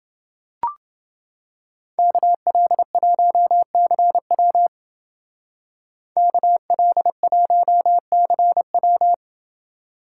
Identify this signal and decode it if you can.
Morse code, heard as a keyed tone, about 28 wpm, 700 Hz. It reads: KL1CW KL1CW